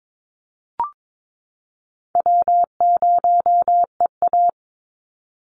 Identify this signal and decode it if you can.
Morse code, heard as a keyed tone, about 22 wpm, 700 Hz. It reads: W0EA
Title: Call Signs Repeated 3x - 22wpm